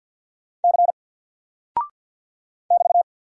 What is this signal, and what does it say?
Morse code, heard as a keyed tone, about 49 wpm, 700 Hz. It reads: C <BT>
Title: Single Character (Letter:Number:Punctuation:Prosign) Alt 50wpm